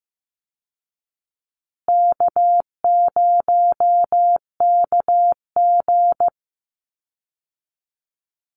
Morse code, heard as a keyed tone, about 14 wpm, 700 Hz. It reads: K0KG
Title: Easy Call Signs Repeated 2x - 15wpm